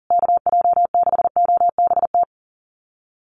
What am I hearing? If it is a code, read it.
Morse code, heard as a keyed tone, about 40 wpm, 700 Hz. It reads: KJ6OBT
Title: USA Call Signs Repeated 3x 40wpm